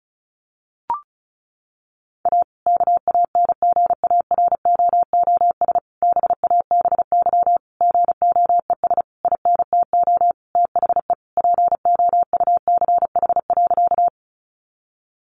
Morse code, heard as a keyed tone, about 34 wpm, 700 Hz. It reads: A KANGAROOS BABY GOES INTO THE POUCH.
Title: Easy Phrases 35wpm